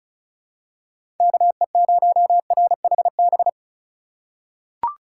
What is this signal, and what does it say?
Morse code, heard as a keyed tone, about 34 wpm, 700 Hz. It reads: KE0RHB
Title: USA Call Signs - 2x Character Spacing - 35wpm